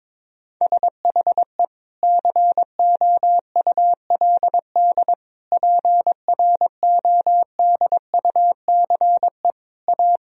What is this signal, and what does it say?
Morse code, heard as a keyed tone, about 22 wpm, 700 Hz. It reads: SHE COULD PRODUCE A
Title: Sentences from Top 1000 Words - Review Part 3 - 22wpm